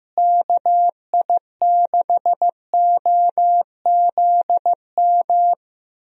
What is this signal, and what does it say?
Morse code, heard as a keyed tone, about 15 wpm, 700 Hz. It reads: KI6OZM
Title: USA Call Signs - 6 Characters - Repeated 3x - 15wpm